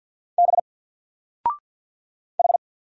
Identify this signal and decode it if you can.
Morse code, heard as a keyed tone, about 50 wpm, 700 Hz. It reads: B H
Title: Single Letter-Number 50wpm